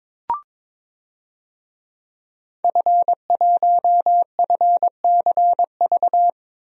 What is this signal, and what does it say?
Morse code, heard as a keyed tone, about 22 wpm, 700 Hz. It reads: F1FCV
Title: DXCC-100 Call Signs - 5 Characters - 22wpm